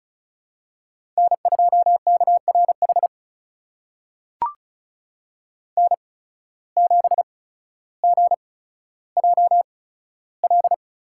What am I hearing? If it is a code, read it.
Morse code, heard as a keyed tone, about 36 wpm, spaced as for 15 wpm, 700 Hz. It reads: N2KRH N7GJL
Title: USA Call Signs - 5 Characters - 8x Character Spacing - 35wpm